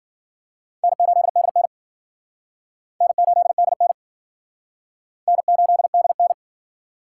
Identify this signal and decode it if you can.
Morse code, heard as a keyed tone, about 49 wpm, 700 Hz. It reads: N8DN N8DN N8DN
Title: Speed Racing - USA 1x2 Call Signs 40wpm